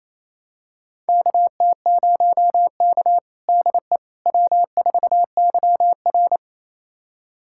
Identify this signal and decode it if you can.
Morse code, heard as a keyed tone, about 27 wpm, 690 Hz. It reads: KT0K DE W4YR
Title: USA Callsign DE Callsign - 4 Characters - 28wpm